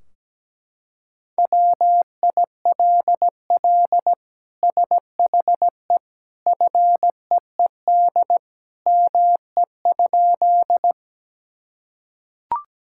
Morse code, heard as a keyed tone, about 17 wpm, 700 Hz. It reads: WILL SHE FEED ME?
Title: Sentences from Top 1600 Words 17wpm